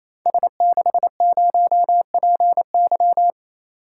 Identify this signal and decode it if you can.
Morse code, heard as a keyed tone, about 29 wpm, 700 Hz. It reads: S60PY
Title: Code Groups - 5 Letter:Numbers - 28wpm